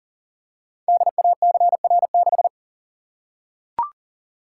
Morse code, heard as a keyed tone, about 41 wpm, 700 Hz. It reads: DACR6
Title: Code Groups - 5 Letter:Numbers - 40wpm